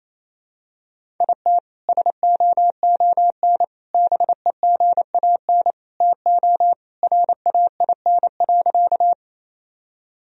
Morse code, heard as a keyed tone, about 28 wpm, 700 Hz. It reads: IT SOON BEGAN TO RAIN.
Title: Sentences from Top 1000 Words - Review Part 1 - 28wpm